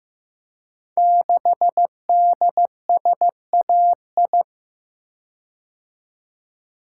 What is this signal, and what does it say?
Morse code, heard as a keyed tone, about 15 wpm, 700 Hz. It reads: 6DSAI
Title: Code Groups - 5 Letter:Numbers - 15wpm